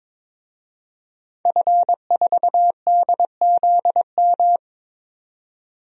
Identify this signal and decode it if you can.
Morse code, heard as a keyed tone, about 22 wpm, 700 Hz. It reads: F4DZM
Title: DXCC-100 Call Signs - 5 Characters - 22wpm